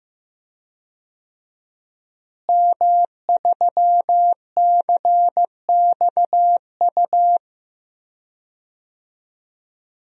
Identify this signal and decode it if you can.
Morse code, heard as a keyed tone, about 15 wpm, 700 Hz. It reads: M3CXU
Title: DXCC-100 Call Signs - 5 Characters - Repeated 3x - 15wpm